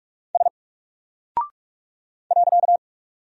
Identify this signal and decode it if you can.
Morse code, heard as a keyed tone, about 44 wpm, 700 Hz. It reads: S .
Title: Single Character (Letter:Number:Punctuation:Prosign) 45wpm